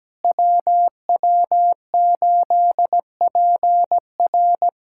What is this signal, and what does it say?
Morse code, heard as a keyed tone, about 17 wpm, 700 Hz. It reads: WW8PR
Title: USA Callsign DE Callsign - 5 Characters - 17wpm